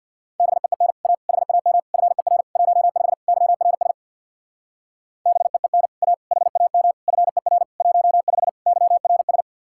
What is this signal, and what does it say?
Morse code, heard as a keyed tone, about 64 wpm, 700 Hz. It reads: BEEN A HAM FER 15 YRS BEEN A HAM FER 15 YRS